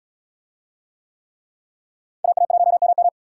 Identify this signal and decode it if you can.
Morse code, heard as a keyed tone, about 74 wpm, 700 Hz. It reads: AA1MM